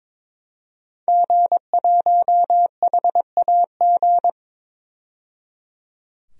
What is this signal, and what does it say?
Morse code, heard as a keyed tone, about 22 wpm, 700 Hz. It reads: G1HAG
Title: DXCC-100 Call Signs - 5 Characters - 22wpm